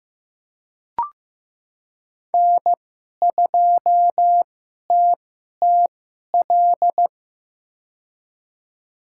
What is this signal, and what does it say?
Morse code, heard as a keyed tone, about 15 wpm, spaced as for 11 wpm, 700 Hz. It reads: N2TTL
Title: USA Call Signs - 5 Characters - 2x Character Spacing - 15wpm